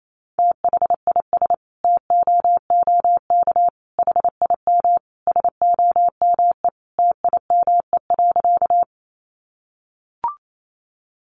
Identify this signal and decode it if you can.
Morse code, heard as a keyed tone, about 28 wpm, 700 Hz. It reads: THIS TOOK HIM SOME TIME.
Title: Sentences from Top 300 Words 28wpm